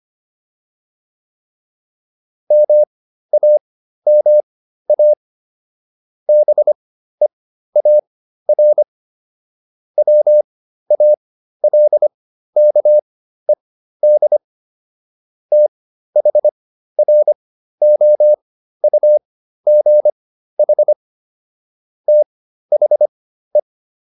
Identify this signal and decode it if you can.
Morse code, heard as a keyed tone, about 25 wpm, spaced as for 13 wpm, 600 Hz. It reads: MAMA BEAR WALKED THROUGH THE